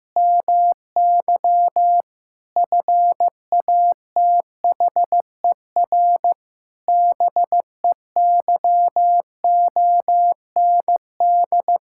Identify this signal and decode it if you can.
Morse code, heard as a keyed tone, about 15 wpm, 700 Hz. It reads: MY FATHER BEYOND
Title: Sentences from Top 1000 Words - Review Part 6 - 15wpm